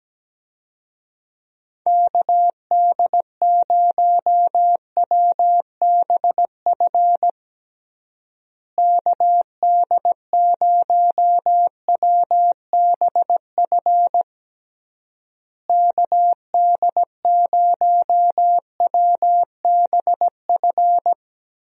Morse code, heard as a keyed tone, about 17 wpm, 700 Hz. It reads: KD0WBF KD0WBF KD0WBF
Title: USA Call Signs - 6 Characters - Repeated 3x - 17wpm